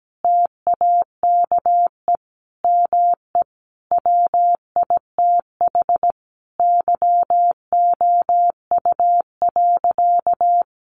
Morse code, heard as a keyed tone, about 17 wpm, 700 Hz. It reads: TAKE ME WITH YOU.
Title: Sentences from Top 200 Words 17wpm